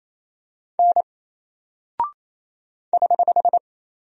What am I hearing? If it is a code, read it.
Morse code, heard as a keyed tone, about 27 wpm, 700 Hz. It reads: N <HH>
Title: Single Character (Letter:Number:Punctuation:Prosign) 28wpm